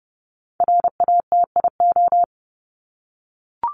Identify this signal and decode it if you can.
Morse code, heard as a keyed tone, about 30 wpm, 700 Hz. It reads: RATIO
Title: Top 1751-2000 Words 30wpm